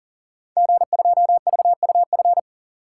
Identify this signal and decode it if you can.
Morse code, heard as a keyed tone, about 41 wpm, 700 Hz. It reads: G2VUF
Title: Code Groups - 5 Letter:Numbers - 40wpm